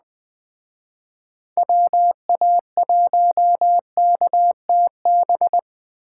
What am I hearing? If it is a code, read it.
Morse code, heard as a keyed tone, about 20 wpm, 700 Hz. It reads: WA1KTB